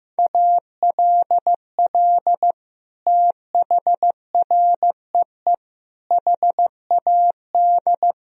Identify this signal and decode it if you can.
Morse code, heard as a keyed tone, about 16 wpm, 700 Hz. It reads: ALL THREE HAD
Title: Sentences from Top 1000 Words - Review Part 2 - 15wpm